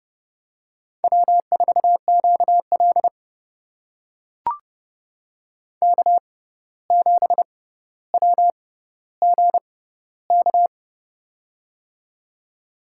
Morse code, heard as a keyed tone, about 30 wpm, spaced as for 14 wpm, 700 Hz. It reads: W4QL K7WGK
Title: USA Call Signs - 6x Character Spacing - 30wpm